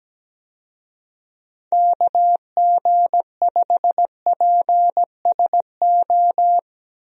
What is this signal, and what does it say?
Morse code, heard as a keyed tone, about 17 wpm, 700 Hz. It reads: KG5PSO
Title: USA Call Signs - 6 Characters - 17wpm